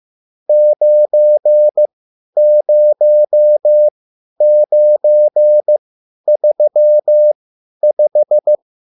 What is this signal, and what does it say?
Morse code, heard as a keyed tone, about 15 wpm, spaced as for 10 wpm, 600 Hz. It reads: 90935